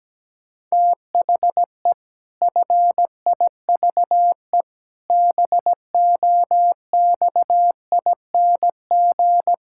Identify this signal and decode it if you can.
Morse code, heard as a keyed tone, about 17 wpm, 700 Hz. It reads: THE FIVE BOXING
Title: Pangram Challenge v2 - 17wpm